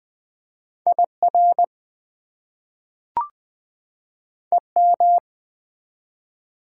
Morse code, heard as a keyed tone, about 20 wpm, 700 Hz. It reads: IR EM
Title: Prefixes and Suffixes 20wpm